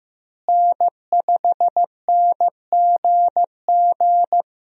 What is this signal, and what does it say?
Morse code, heard as a keyed tone, about 15 wpm, 700 Hz. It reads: N5NGG
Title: USA Call Signs - 5 Characters - Repeated 3x - 15wpm